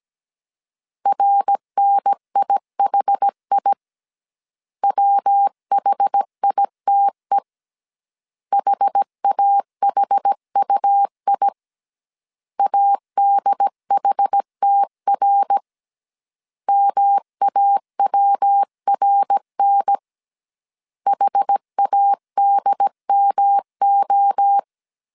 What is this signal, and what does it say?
Morse code, heard as a keyed tone, about 17 wpm, 800 Hz. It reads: RNIHI WHITE HAHUI ADHTR MAWRN HADMO
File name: CW / DET / 1 3 1500C S A 16wpm 2141